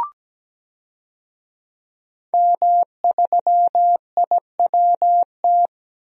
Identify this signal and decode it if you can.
Morse code, heard as a keyed tone, about 17 wpm, 700 Hz. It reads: M3IWT
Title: DXCC-100 Call Signs - 5 Characters - Repeated 3x - 17wpm